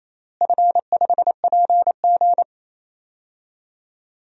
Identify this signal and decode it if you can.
Morse code, heard as a keyed tone, about 28 wpm, 700 Hz. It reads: F5PG
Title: DXCC-100 Call Signs - 4 Characters - Repeated 3x - 28wpm